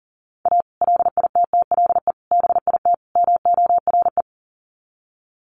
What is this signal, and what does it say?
Morse code, heard as a keyed tone, about 38 wpm, 700 Hz. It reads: A LITTLE BIT MORE
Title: Sets of 4 Words 40wpm